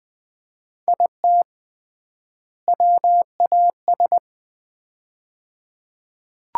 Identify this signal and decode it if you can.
Morse code, heard as a keyed tone, about 20 wpm, 700 Hz. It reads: IT WAS
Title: 3x Word-Spacing - Sets of 2 Words - 20wpm